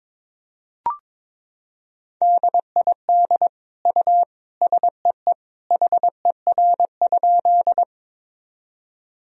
Morse code, heard as a keyed tone, about 22 wpm, 700 Hz. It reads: DID U SEE HER?
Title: Sentences from Top 100 Words and Abbreviations 22wpm